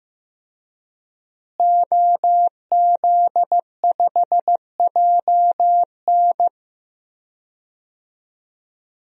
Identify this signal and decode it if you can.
Morse code, heard as a keyed tone, about 16 wpm, 700 Hz. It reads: OZ5JN
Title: DXCC-100 Call Signs - 5 Characters - Repeated 3x - 15wpm